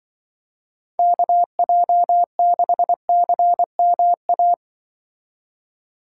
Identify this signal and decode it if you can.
Morse code, heard as a keyed tone, about 24 wpm, 700 Hz. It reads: KJ6CMA
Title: Speed Racing - USA 2x3 Call Signs 20wpm